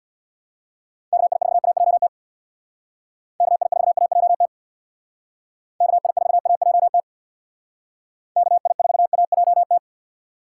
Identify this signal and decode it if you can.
Morse code, heard as a keyed tone, about 63 wpm, 700 Hz. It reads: KI4AJT KI4AJT KI4AJT KI4AJT